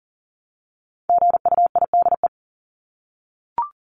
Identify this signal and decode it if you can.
Morse code, heard as a keyed tone, about 39 wpm, 700 Hz. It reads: GUIDE
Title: Top 501-750 Words 40wpm